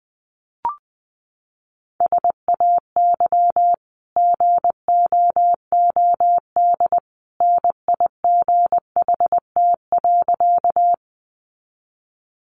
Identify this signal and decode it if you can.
Morse code, heard as a keyed tone, about 20 wpm, 700 Hz. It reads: SAY GOOD NIGHT.
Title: Sentences from Top 300 Words 20wpm